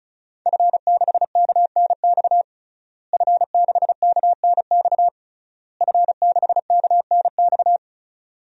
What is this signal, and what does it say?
Morse code, heard as a keyed tone, about 34 wpm, 690 Hz. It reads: F6KNX F6KNX F6KNX
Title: DXCC-100 Call Signs - 5 Characters - Repeated 3x - 35wpm